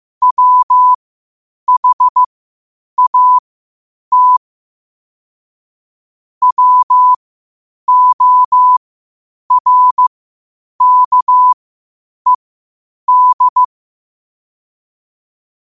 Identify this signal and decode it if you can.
Morse code, heard as a keyed tone, about 15 wpm, spaced as for 8 wpm, 1000 Hz. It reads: WHAT WORKED